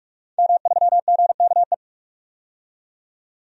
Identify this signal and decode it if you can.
Morse code, heard as a keyed tone, about 46 wpm, 700 Hz. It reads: M3GKE
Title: DXCC-100 Call Signs - 5 Characters - 45wpm